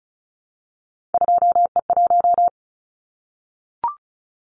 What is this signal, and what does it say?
Morse code, heard as a keyed tone, about 34 wpm, 700 Hz. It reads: 2E1
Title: Callsign Prefixes for Top 25 DXCC - Repeated 3x - 35wpm